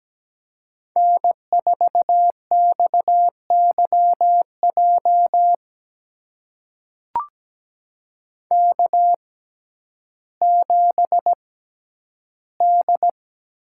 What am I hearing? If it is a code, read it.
Morse code, heard as a keyed tone, about 17 wpm, spaced as for 10 wpm, 700 Hz. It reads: N4XYJ K7D
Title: USA Call Signs - 5 Characters - 6x Character Spacing - 17wpm